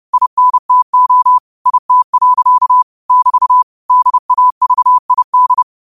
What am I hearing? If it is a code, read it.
Morse code, heard as a keyed tone, about 30 wpm, 1000 Hz. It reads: INTO IT. <BT> DAVID
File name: qotdfiles / 30wpm